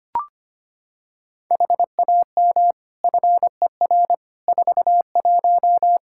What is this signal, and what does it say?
Morse code, heard as a keyed tone, about 25 wpm, 700 Hz. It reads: HAM FER 41
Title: Practice QSO - Exchange 3 - More Info - 25wpm